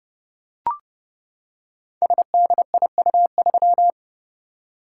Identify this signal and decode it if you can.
Morse code, heard as a keyed tone, about 30 wpm, 700 Hz. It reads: SDIU3
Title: Code Groups - 5 Letter:Numbers - 30wpm